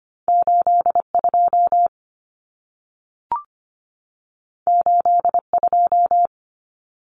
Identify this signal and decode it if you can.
Morse code, heard as a keyed tone, about 25 wpm, 700 Hz. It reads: 82 82
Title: Common Cut Numbers - 2-Digit 25wpm